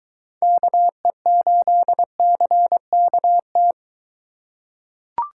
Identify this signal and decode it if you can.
Morse code, heard as a keyed tone, about 22 wpm, 700 Hz. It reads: KE8CKT